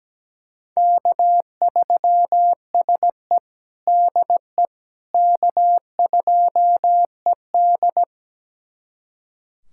Morse code, heard as a keyed tone, about 17 wpm, 700 Hz. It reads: K3SE DE K2ED
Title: USA Callsign DE Callsign - 4 Characters - 17wpm